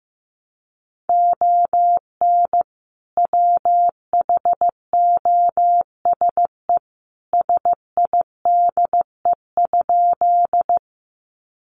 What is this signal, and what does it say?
Morse code, heard as a keyed tone, about 15 wpm, 700 Hz. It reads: ON WHOSE SIDE?
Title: Sentences from Top 1200 Words 15wpm